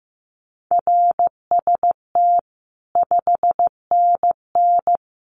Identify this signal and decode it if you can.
Morse code, heard as a keyed tone, about 15 wpm, 700 Hz. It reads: RST 5NN
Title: Common QSO Phrases 15wpm